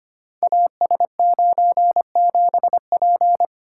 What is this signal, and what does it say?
Morse code, heard as a keyed tone, about 25 wpm, 700 Hz. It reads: AS97P
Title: Code Groups - 5 Letter:Numbers - 25wpm